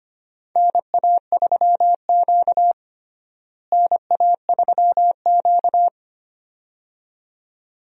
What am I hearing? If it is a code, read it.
Morse code, heard as a keyed tone, about 25 wpm, 700 Hz. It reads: NA3Q NA3Q
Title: USA Call Signs - 4 Characters - Repeated 2x - 25wpm